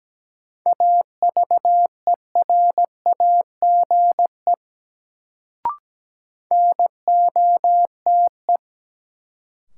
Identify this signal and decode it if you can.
Morse code, heard as a keyed tone, about 17 wpm, 700 Hz. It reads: AVERAGE NOTE
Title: Sets of 5 Words - New Word Drill - 17wpm